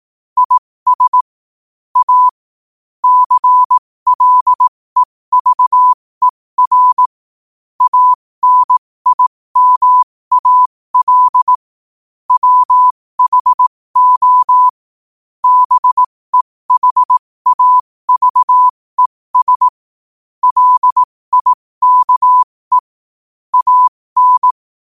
Morse code, heard as a keyed tone, about 18 wpm, spaced as for 15 wpm, 1000 Hz. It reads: IS A CLEVER ANIMAL WHO BEHAVES LIKE AN
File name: qotdfiles / 15wpm